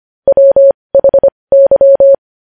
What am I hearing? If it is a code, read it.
Morse code, heard as a keyed tone, about 25 wpm, spaced as for 20 wpm, 550 Hz. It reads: WHY
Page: News Headlines in Morse Code at 20 WPM